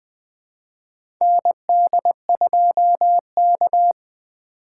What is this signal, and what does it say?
Morse code, heard as a keyed tone, about 20 wpm, 700 Hz. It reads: ND2K
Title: USA Call Signs - 4 Characters - 2x Character Spacing - 20wpm